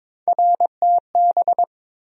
Morse code, heard as a keyed tone, about 22 wpm, 700 Hz. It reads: RTB
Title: USA Call Sign Postfixes - 3 Letters 22wpm